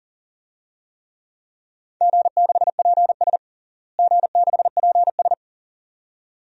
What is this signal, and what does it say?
Morse code, heard as a keyed tone, about 40 wpm, 700 Hz. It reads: G6PS G6PS
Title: DXCC-100 Call Signs - 4 Characters - Repeated 2x - 40wpm